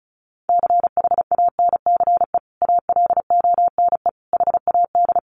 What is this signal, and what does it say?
Morse code, heard as a keyed tone, about 34 wpm, 700 Hz. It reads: CHANCE ALONE HAD
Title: Sentences from Top 1000 Words 35wpm